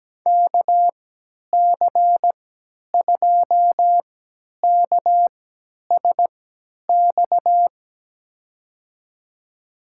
Morse code, heard as a keyed tone, about 17 wpm, spaced as for 10 wpm, 700 Hz. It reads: KC2KSX
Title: USA Call Signs - 6 Characters - 3x Character Spacing - 17wpm